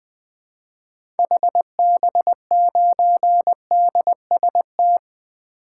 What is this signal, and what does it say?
Morse code, heard as a keyed tone, about 21 wpm, 700 Hz. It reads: HB9DST
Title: 2023 SOTA Activator Call Signs - 20wpm